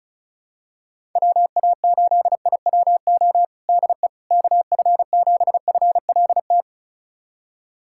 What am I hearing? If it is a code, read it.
Morse code, heard as a keyed tone, about 36 wpm, 700 Hz. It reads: WA8IWO DE KF7FLT
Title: USA Callsign DE Callsign 35wpm